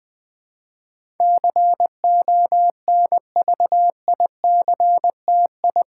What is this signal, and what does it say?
Morse code, heard as a keyed tone, about 20 wpm, 700 Hz. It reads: CONVICTI
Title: Build A Word - 4 Letters At A Time - 10-Letter Words - 20wpm